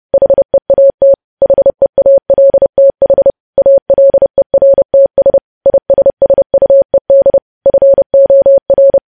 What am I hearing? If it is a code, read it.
Morse code, heard as a keyed tone, about 30 wpm, 550 Hz. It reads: HEAT HEALTH ALERTS ISSUED FOR
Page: news headlines in morse code at 30 wpm